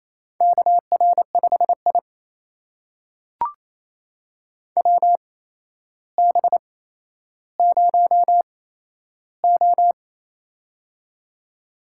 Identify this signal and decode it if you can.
Morse code, heard as a keyed tone, about 28 wpm, spaced as for 12 wpm, 700 Hz. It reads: KR5I WB0O
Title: USA Call Signs - 4 Characters - 8x Character Spacing - 28wpm